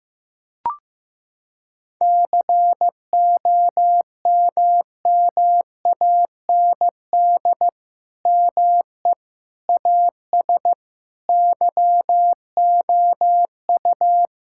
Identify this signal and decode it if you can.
Morse code, heard as a keyed tone, about 15 wpm, 700 Hz. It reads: COMMAND ME AS YOU
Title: Sentences from Top 1600 Words 15wpm